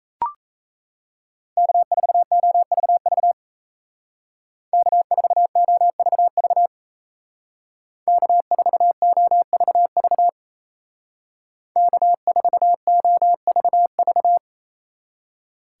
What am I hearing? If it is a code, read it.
Morse code, heard as a keyed tone, about 36 wpm, 700 Hz. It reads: K4OVV K4OVV K4OVV K4OVV